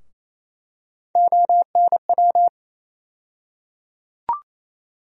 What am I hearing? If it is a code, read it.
Morse code, heard as a keyed tone, about 28 wpm, 700 Hz. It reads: ONW
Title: USA Call Sign Postfixes - 3 Letters 28wpm